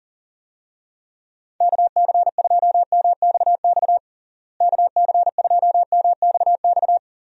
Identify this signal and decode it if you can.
Morse code, heard as a keyed tone, about 40 wpm, 700 Hz. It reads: KC2MXX KC2MXX
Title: USA Call Signs - 6 Characters - Repeated 3x - 40wpm